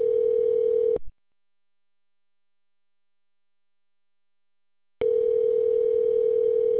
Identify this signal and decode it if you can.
Ringback tone (North America), 440 + 480 Hz.